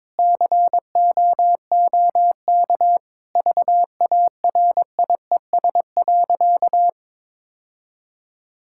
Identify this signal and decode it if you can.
Morse code, heard as a keyed tone, about 22 wpm, 700 Hz. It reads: COOK VARIES.
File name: Sentences with 7-Word Phrases - 22wpm